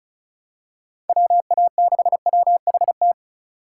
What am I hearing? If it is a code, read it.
Morse code, heard as a keyed tone, about 34 wpm, 700 Hz. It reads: WA6WHT
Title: USA Call Signs - 6 Characters - 3x Character Spacing - 35wpm